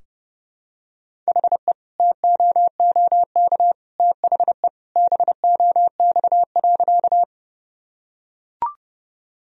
Morse code, heard as a keyed tone, about 30 wpm, 700 Hz. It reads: HE TOOK THE BOX.
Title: Sentences from Top 1000 Words - Review Part 2 - 30wpm